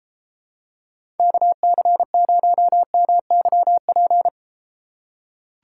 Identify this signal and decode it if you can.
Morse code, heard as a keyed tone, about 31 wpm, 700 Hz. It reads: KC0MYP